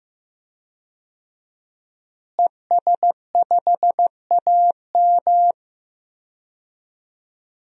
Morse code, heard as a keyed tone, about 15 wpm, 700 Hz. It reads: ES5AM
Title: DXCC-100 Call Signs - 5 Characters - Repeated 3x - 15wpm